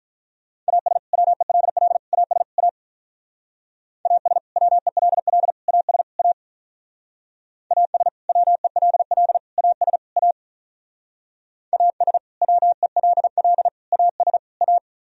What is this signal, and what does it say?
Morse code, heard as a keyed tone, about 45 wpm, 700 Hz. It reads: AS WELL AS A AS WELL AS A AS WELL AS A AS WELL AS A